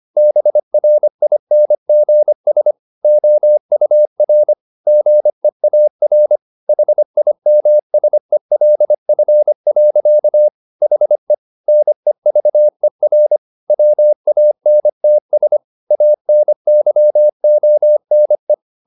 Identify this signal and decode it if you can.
Morse code, heard as a keyed tone, about 25 wpm, 600 Hz. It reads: BRINGS OUR GEAR HIMSELF. HE NEVER WANTS ANYONE
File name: the old man and the sea part 1 25wpm